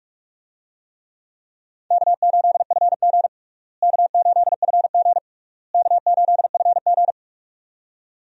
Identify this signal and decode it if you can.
Morse code, heard as a keyed tone, about 46 wpm, 700 Hz. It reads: K8FG K8FG K8FG
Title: USA Call Signs - 4 Characters - Repeated 3x - 45wpm